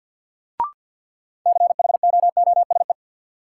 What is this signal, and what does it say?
Morse code, heard as a keyed tone, about 47 wpm, 700 Hz. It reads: CHOOSE
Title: Sentences from Top 700 Words - New Word Drill - 50wpm